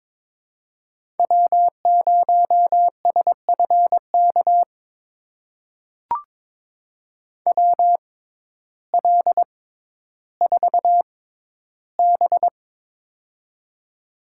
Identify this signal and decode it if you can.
Morse code, heard as a keyed tone, about 22 wpm, spaced as for 12 wpm, 700 Hz. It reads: W0SFK WL4B
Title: USA Call Signs - 6x Character Spacing - 22wpm